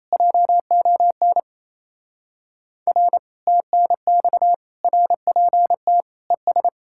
Morse code, heard as a keyed tone, about 33 wpm, 700 Hz. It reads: JON R TNX RPT ES
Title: Speed-Racing Common QSO Phrases Encore - 28wpm